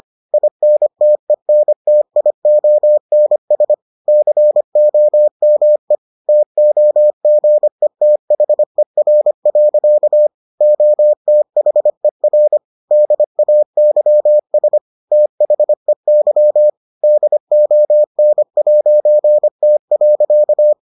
Morse code, heard as a keyed tone, about 25 wpm, 600 Hz. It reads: INTENTIONS COME TOGETHER. OTHER DAYS THEY DON'T.